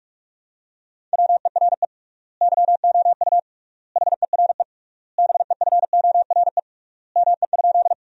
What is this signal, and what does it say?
Morse code, heard as a keyed tone, about 42 wpm, 700 Hz. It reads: WERE YOU HERE BEFORE ME?
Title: 3x Word-Spacing - Sentences from Top 200 Words - 45wpm